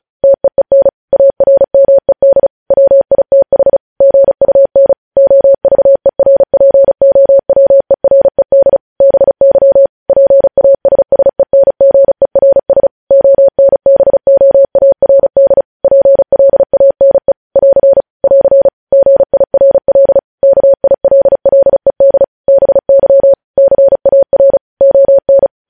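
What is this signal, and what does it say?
Morse code, heard as a keyed tone, about 36 wpm, 550 Hz. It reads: TEEN ARMED WITH GUN OVERPOWERED BY PASSENGERS ONBOARD PLANE <AR> <AR> GIRL KILLED BY CAR ON
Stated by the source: News headlines sent in Morse code at 35 WPM.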